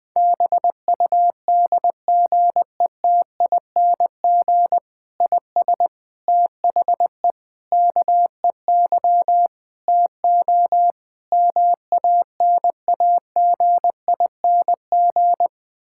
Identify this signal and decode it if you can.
Morse code, heard as a keyed tone, about 21 wpm, 700 Hz. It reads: BUDGETING IS THE KEY TO MANAGING
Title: Difficult Sentences with Hints - 20wpm